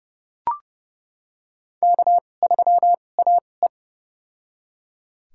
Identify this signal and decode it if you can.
Morse code, heard as a keyed tone, about 32 wpm, spaced as for 22 wpm, 700 Hz. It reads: K3AE
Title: USA Call Signs - 4 Characters - 2x Character Spacing - 30wpm